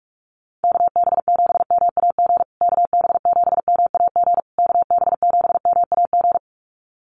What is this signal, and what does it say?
Morse code, heard as a keyed tone, about 44 wpm, 700 Hz. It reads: KB7MAG KB7MAG KB7MAG
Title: USA Call Signs Repeated 3x 45wpm